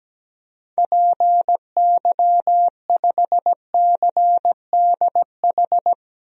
Morse code, heard as a keyed tone, about 17 wpm, 700 Hz. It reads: PY5CDH
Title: DXCC-100 Call Signs - 6 Characters - Repeated 3x - 17wpm